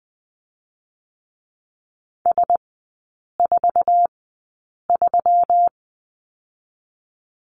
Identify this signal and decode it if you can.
Morse code, heard as a keyed tone, about 20 wpm, 700 Hz. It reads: S 4 3
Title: 205 Lesson 33e - Letter j - Three Characters 20wpm